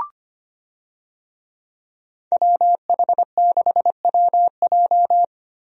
Morse code, heard as a keyed tone, about 25 wpm, 700 Hz. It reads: WH6WJ
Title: USA Call Signs - 5 Characters - 25wpm